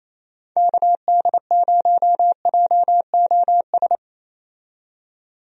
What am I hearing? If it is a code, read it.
Morse code, heard as a keyed tone, about 28 wpm, 700 Hz. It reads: KD0JOS
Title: USA Call Signs - 6 Characters - 28wpm